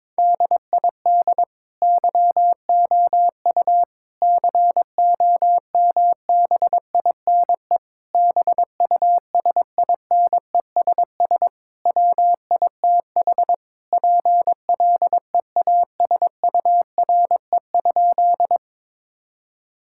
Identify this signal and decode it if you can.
Morse code, heard as a keyed tone, about 22 wpm, 700 Hz. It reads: DID YOU COMBINE BUSINESS WITH PLEASURE?
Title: Sentences from Top 1600 Words 22wpm